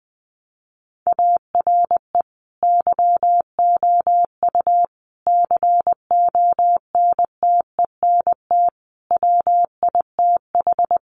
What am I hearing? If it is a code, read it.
Morse code, heard as a keyed tone, about 20 wpm, 700 Hz. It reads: ARE YOU CONTENT WITH
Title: Sentences from Top 900 Words 20wpm